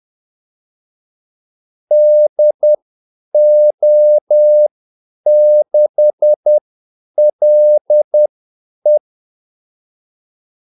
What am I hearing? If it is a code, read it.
Morse code, heard as a keyed tone, about 10 wpm, spaced as for 8 wpm, 600 Hz. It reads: DO6LE
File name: Callsigns-10-8wpm